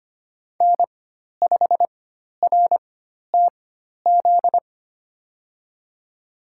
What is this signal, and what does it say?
Morse code, heard as a keyed tone, about 25 wpm, spaced as for 12 wpm, 700 Hz. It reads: N5RTZ
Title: USA Call Signs - 5 Characters - 4x Character Spacing - 25wpm